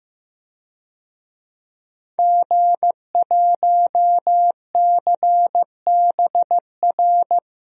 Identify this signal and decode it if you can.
Morse code, heard as a keyed tone, about 15 wpm, 700 Hz. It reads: G1CBR
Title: DXCC-100 Call Signs - 5 Characters - Repeated 3x - 15wpm